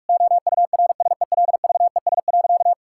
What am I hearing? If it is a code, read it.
Morse code, heard as a keyed tone, about 46 wpm, 700 Hz. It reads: OURSELVES.
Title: Sentences from Top 1600 Words 45wpm